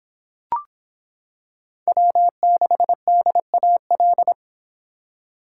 Morse code, heard as a keyed tone, about 25 wpm, 700 Hz. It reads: W6DAL